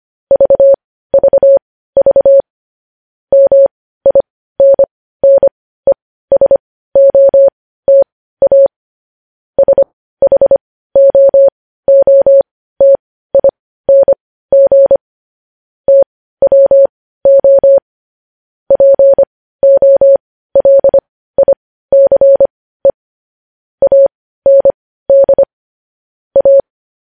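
Morse code, heard as a keyed tone, about 25 wpm, spaced as for 15 wpm, 550 Hz. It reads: VVV MINNESOTA SHOOTING TWO POLICE AND A
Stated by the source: Morse code news headlines sent at 15 words per minute.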